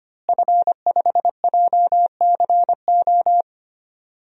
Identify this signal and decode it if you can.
Morse code, heard as a keyed tone, about 25 wpm, 700 Hz. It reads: F5JCO
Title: DXCC-100 Call Signs - 5 Characters - Repeated 3x - 25wpm